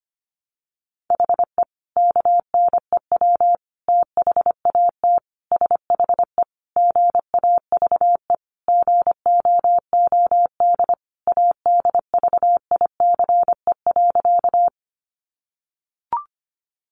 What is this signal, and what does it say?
Morse code, heard as a keyed tone, about 25 wpm, 700 Hz. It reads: HE KNEW THAT SHE GAVE GOOD ADVICE.
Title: Sentences from Top 1100 Words 25wpm